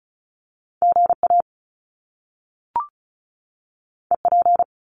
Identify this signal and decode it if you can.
Morse code, heard as a keyed tone, about 36 wpm, 700 Hz. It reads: GA EP
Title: Top Two Letter Combinations in Words 35wpm